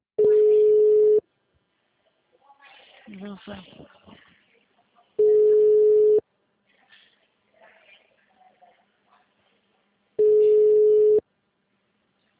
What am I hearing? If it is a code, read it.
Ringback tone (Europe, 425 Hz), with speech.